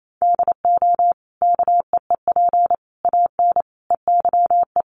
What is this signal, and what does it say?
Morse code, heard as a keyed tone, about 28 wpm, 700 Hz. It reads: DO KEEP AN EYE
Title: Sentences from Top 1100 Words 28wpm